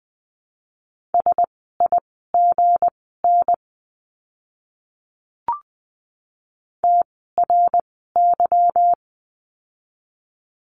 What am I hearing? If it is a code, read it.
Morse code, heard as a keyed tone, about 20 wpm, spaced as for 15 wpm, 700 Hz. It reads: SIGN TRY
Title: 160 Lesson 26f - Letter g - Words 20wpm